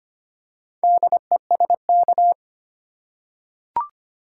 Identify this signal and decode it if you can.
Morse code, heard as a keyed tone, about 25 wpm, 700 Hz. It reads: DESK
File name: practice / 25wpm / Easy Words with Hints - 25wpm